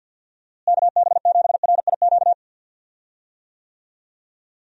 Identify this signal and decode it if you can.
Morse code, heard as a keyed tone, about 51 wpm, 700 Hz. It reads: KB7RIQ
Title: USA Call Signs - 6 Characters - Repeated 3x - 50wpm